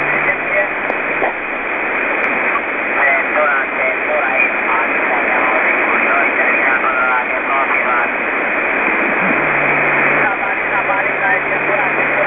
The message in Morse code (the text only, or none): none